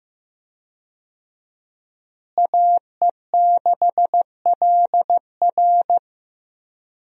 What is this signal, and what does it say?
Morse code, heard as a keyed tone, about 15 wpm, 700 Hz. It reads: AE6LR
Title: USA Call Signs - 5 Characters - Repeated 2x - 15wpm